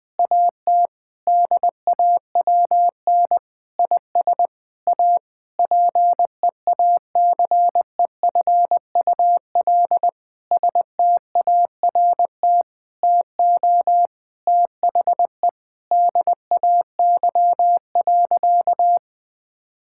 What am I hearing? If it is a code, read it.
Morse code, heard as a keyed tone, about 21 wpm, 700 Hz. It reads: AT DAWN IS A PEACEFUL START TO THE DAY.
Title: Difficult Sentences with Hints - 20wpm